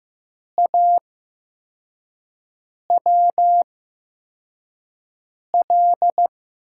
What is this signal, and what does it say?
Morse code, heard as a keyed tone, about 15 wpm, spaced as for 4 wpm, 700 Hz. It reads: AWL
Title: USA Call Signs - 6 Characters - 8x Character Spacing - 15wpm